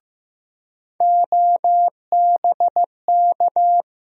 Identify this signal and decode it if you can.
Morse code, heard as a keyed tone, about 15 wpm, 700 Hz. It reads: OBK
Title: USA Call Sign Postfixes - 3 Letters 15wpm